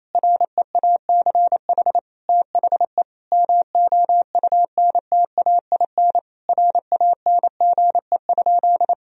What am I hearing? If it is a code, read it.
Morse code, heard as a keyed tone, about 28 wpm, 700 Hz. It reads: REACH THE MOUNTAIN RANGE?
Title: Sentences from Top 1000 Words - Review Part 4 - 28wpm